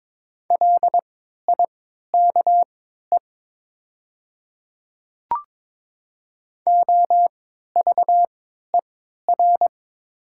Morse code, heard as a keyed tone, about 21 wpm, spaced as for 12 wpm, 700 Hz. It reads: LIKE OVER
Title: 3x Character Spacing - Top 100 Words - 22wpm